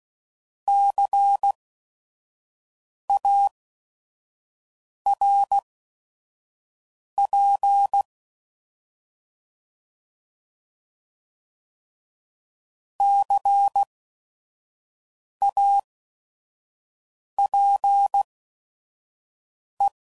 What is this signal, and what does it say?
Morse code, heard as a keyed tone, about 16 wpm, spaced as for 5 wpm, 800 Hz. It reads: CARP CAPE
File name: Prefixes / By First Letter / Cs